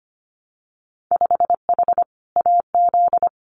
Morse code, heard as a keyed tone, about 25 wpm, 700 Hz. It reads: 5H AZ
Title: Winter Field Day Exchanges Encore 25wpm